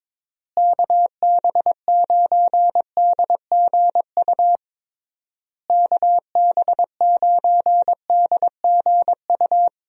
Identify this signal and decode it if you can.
Morse code, heard as a keyed tone, about 22 wpm, 700 Hz. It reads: KB9DGU KB9DGU
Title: USA Call Signs - 6 Characters - Repeated 3x - 22wpm